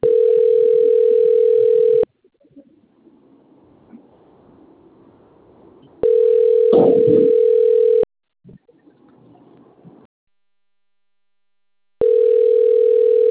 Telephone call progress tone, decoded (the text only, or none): ringback tone